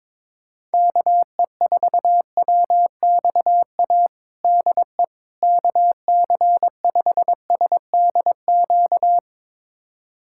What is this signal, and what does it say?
Morse code, heard as a keyed tone, about 22 wpm, 700 Hz. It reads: KE4WXA DE KC5SDQ
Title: USA Callsign DE Callsign - 6 Characters - 22wpm